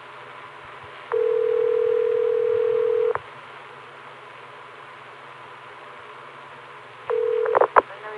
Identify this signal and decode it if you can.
Ringback tone (North America), 440 + 480 Hz.